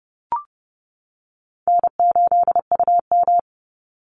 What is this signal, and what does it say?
Morse code, heard as a keyed tone, about 30 wpm, 700 Hz. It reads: N8UM
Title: Call Signs - 30wpm